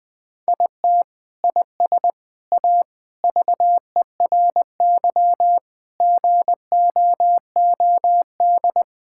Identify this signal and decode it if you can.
Morse code, heard as a keyed tone, about 20 wpm, 700 Hz. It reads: IT IS A VERY GOOD
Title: Sentences with 3-Word Phrases - Spoken Beginning - 20wpm